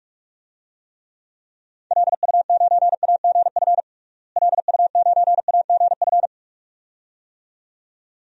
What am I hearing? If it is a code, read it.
Morse code, heard as a keyed tone, about 41 wpm, 700 Hz. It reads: LU9AGF LU9AGF
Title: DXCC-100 Call Signs - 6 Characters - Repeated 2x - 45wpm